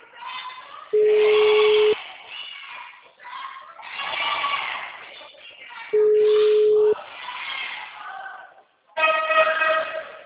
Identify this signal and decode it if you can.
Ringback tone (Europe, 425 Hz).
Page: Nova Odessa